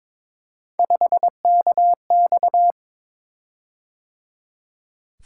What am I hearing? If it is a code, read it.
Morse code, heard as a keyed tone, about 22 wpm, 700 Hz. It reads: 5KX
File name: practice / 22wpm / Three Letter-Number Combinations - 22wpm